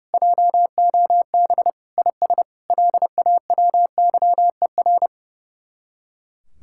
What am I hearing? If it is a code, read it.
Morse code, heard as a keyed tone, about 34 wpm, 700 Hz. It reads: JOB IS LAWYER
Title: Speed-Racing Common QSO Phrases Encore - 30wpm